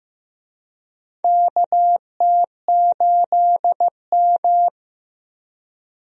Morse code, heard as a keyed tone, about 16 wpm, 700 Hz. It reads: KT8M
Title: USA Call Signs - 4 Characters - Repeated 3x - 15wpm